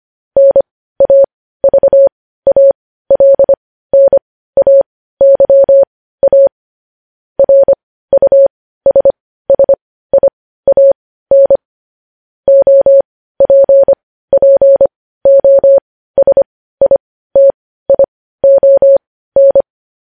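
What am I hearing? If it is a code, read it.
Morse code, heard as a keyed tone, about 25 wpm, spaced as for 15 wpm, 550 Hz. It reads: NAVALNAYA RUSSIAN OPPOSITION
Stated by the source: News headlines sent in Morse code at 15 wpm.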